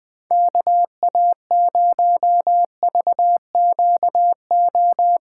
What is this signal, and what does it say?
Morse code, heard as a keyed tone, about 20 wpm, 700 Hz. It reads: KA0VQO
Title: USA Call Signs - 6 Characters - 2x Character Spacing - 20wpm